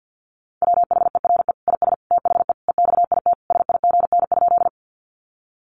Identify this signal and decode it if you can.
Morse code, heard as a keyed tone, about 50 wpm, 700 Hz. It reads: WHERE IS THE EXIT SIGN?
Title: Sentences from Top 1100 Words 50wpm